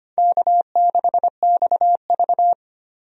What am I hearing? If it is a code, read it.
Morse code, heard as a keyed tone, about 25 wpm, 700 Hz. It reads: K6XV
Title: USA Call Signs - 4 Characters - 25wpm